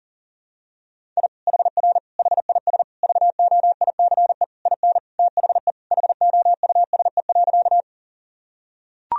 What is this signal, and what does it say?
Morse code, heard as a keyed tone, about 40 wpm, 700 Hz. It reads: I HR HIS VOICE IN THE HOUSE.
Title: Sentences from Top 500 Words and Abbreviations 40wpm